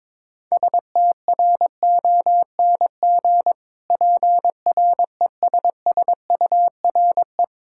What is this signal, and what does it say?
Morse code, heard as a keyed tone, about 21 wpm, 700 Hz. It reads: STRONG PRESSURE
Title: Sentences from Top 1000 Words - Review Part 6 - 22wpm